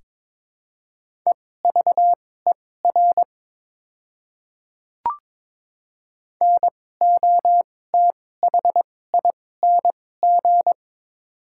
Morse code, heard as a keyed tone, about 22 wpm, spaced as for 16 wpm, 700 Hz. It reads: EVER NOTHING
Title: 2x Character Spacing - Top 500 Words - 22wpm